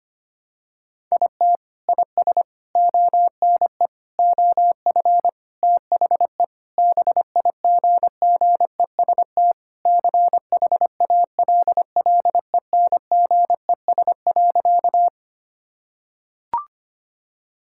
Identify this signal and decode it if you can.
Morse code, heard as a keyed tone, about 25 wpm, 700 Hz. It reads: IT IS ONE OF THE BIGGEST CHALLENGES.
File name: practice / 25wpm / Sentences with 4-Word Phrases - 25wpm